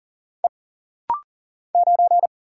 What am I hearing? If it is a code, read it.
Morse code, heard as a keyed tone, about 40 wpm, 700 Hz. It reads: E 9
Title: Single Letter-Number Alt - Rapid-Fire 40wpm